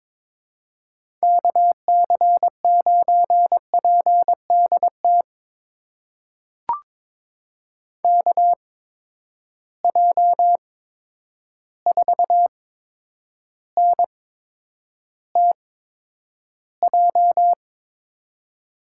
Morse code, heard as a keyed tone, about 22 wpm, spaced as for 9 wpm, 700 Hz. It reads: KC9PDT KJ4NTJ
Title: USA Call Signs - 8x Character Spacing - 22wpm